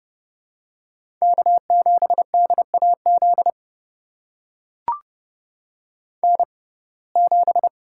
Morse code, heard as a keyed tone, about 28 wpm, spaced as for 21 wpm, 700 Hz. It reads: K7DAZ N7
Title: USA Call Signs - 5 Characters - 6x Character Spacing - 30wpm